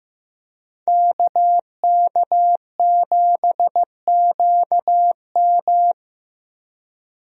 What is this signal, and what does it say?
Morse code, heard as a keyed tone, about 15 wpm, 700 Hz. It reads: KK7QM
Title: USA Call Signs - 5 Characters - Repeated 3x - 15wpm